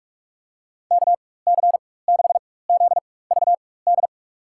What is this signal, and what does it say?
Morse code, heard as a keyed tone, about 41 wpm, spaced as for 20 wpm, 700 Hz. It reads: KC6ZVD
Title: USA Call Signs - 6 Characters - 4x Character Spacing - 45wpm